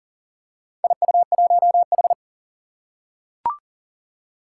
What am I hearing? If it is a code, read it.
Morse code, heard as a keyed tone, about 40 wpm, 700 Hz. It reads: IU1H
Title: Four Challenging Letter-Number Combinations 40wpm